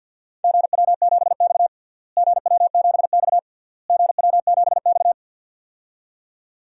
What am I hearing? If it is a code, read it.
Morse code, heard as a keyed tone, about 47 wpm, 700 Hz. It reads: GW7X GW7X GW7X
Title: DXCC-100 Call Signs - 4 Characters - Repeated 3x - 50wpm